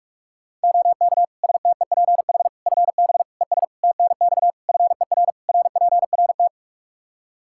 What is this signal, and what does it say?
Morse code, heard as a keyed tone, about 46 wpm, 700 Hz. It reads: OK STEPH FB ES TNX FER RPRT
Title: Practice QSO - Exchange 2 - Info - 45wpm